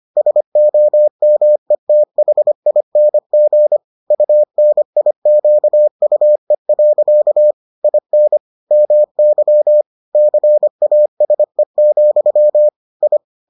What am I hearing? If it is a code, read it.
Morse code, heard as a keyed tone, about 25 wpm, 600 Hz. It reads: SOMETHING UNIQUE. IN MY CASE, I